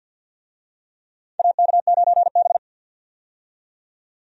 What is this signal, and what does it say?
Morse code, heard as a keyed tone, about 49 wpm, 700 Hz. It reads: AK9B
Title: USA Call Signs - 4 Characters - 50wpm